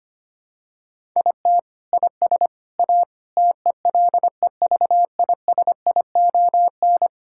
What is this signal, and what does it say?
Morse code, heard as a keyed tone, about 25 wpm, 700 Hz. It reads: IT IS A TELEVISION
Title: Sentences from Top 1000 Words - Review Part 5 - 25wpm